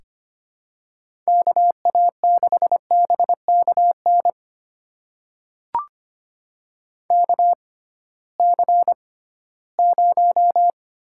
Morse code, heard as a keyed tone, about 25 wpm, spaced as for 16 wpm, 700 Hz. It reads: KA6BKN KC0
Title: USA Call Signs - 6 Characters - 6x Character Spacing - 25wpm